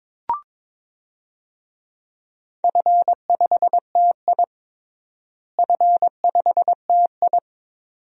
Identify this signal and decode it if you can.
Morse code, heard as a keyed tone, about 22 wpm, 700 Hz. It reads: F5TI F5TI
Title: DXCC-100 Call Signs - 4 Characters - Repeated 2x - 22wpm